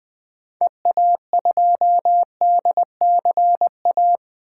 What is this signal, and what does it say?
Morse code, heard as a keyed tone, about 20 wpm, 700 Hz. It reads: EA2DCA
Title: 2023 SOTA Activator Call Signs - 20wpm